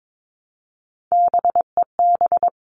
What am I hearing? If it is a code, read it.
Morse code, heard as a keyed tone, about 22 wpm, 700 Hz. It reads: BEB
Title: All Cut Numbers - 3-Digit 22wpm